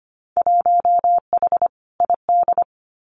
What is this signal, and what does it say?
Morse code, heard as a keyed tone, about 25 wpm, 700 Hz. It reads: 1H ID
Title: Winter Field Day Exchanges 25wpm